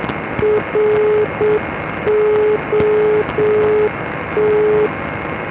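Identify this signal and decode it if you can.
Morse code, heard as a keyed tone, about 7 wpm, 420 Hz, with noise. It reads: ROT
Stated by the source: Non-directional beacon (NDB) Morse ident.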